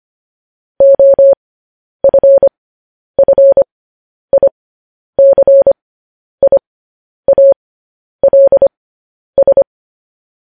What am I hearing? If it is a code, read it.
Morse code, heard as a keyed tone, about 25 wpm, spaced as for 10 wpm, 550 Hz. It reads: OFFICIALS
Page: news-headlines-in-morse-code-at-10-wpm